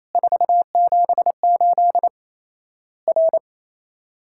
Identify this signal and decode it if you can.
Morse code, heard as a keyed tone, about 28 wpm, 700 Hz. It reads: 478 R
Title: Worked All Europe DX Contest - QTC Exchange - 28wpm